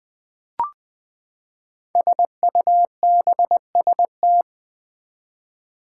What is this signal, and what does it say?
Morse code, heard as a keyed tone, about 20 wpm, 700 Hz. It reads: SUBST
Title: Build A Word - 5 Letters At A Time - 12-Letter Words - 20wpm